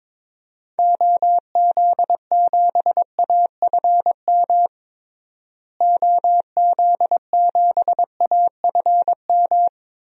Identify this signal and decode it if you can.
Morse code, heard as a keyed tone, about 22 wpm, 700 Hz. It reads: OZ7AFM OZ7AFM
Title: DXCC-100 Call Signs - 6 Characters - Repeated 3x - 22wpm